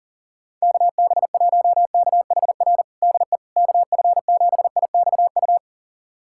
Morse code, heard as a keyed tone, about 37 wpm, 700 Hz. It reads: KB1KHR DE KF7IXU
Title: USA Callsign DE Callsign - 6 Characters - 40wpm